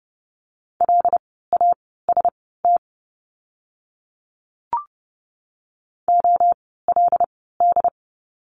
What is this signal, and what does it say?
Morse code, heard as a keyed tone, about 30 wpm, spaced as for 17 wpm, 700 Hz. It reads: LAST OLD
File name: course / 30:10wpm / 237 Lesson 38f - Letter q - Words 30wpm